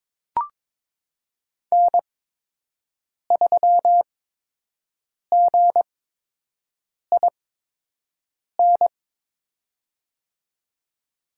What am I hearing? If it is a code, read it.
Morse code, heard as a keyed tone, about 21 wpm, spaced as for 6 wpm, 700 Hz. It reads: N3GIN
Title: USA Call Signs - 5 Characters - 8x Character Spacing - 22wpm